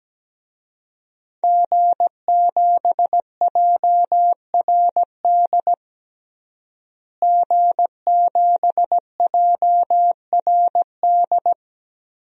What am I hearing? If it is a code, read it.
Morse code, heard as a keyed tone, about 17 wpm, 700 Hz. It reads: G7JRD G7JRD
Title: DXCC-100 Call Signs - 5 Characters - Repeated 3x - 17wpm